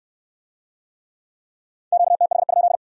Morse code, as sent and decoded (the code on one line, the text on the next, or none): --.- - .... ..--..
QTH?